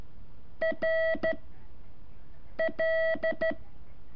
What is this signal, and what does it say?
Morse code, heard as a keyed tone, about 13 wpm, 650 Hz. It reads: R L
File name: curso / ejercicio 4